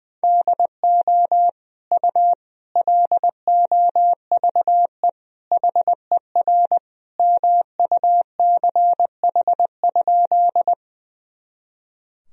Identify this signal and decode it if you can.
Morse code, heard as a keyed tone, about 20 wpm, 700 Hz. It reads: DO U LOVE HER MUCH?
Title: Sentences from Top 400 Words and Abbreviations 20wpm